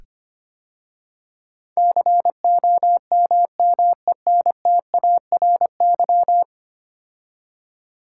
Morse code, heard as a keyed tone, about 25 wpm, 700 Hz. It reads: COMMENTARY
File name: 10 Letter Words - First 3 Letters Given - 25wpm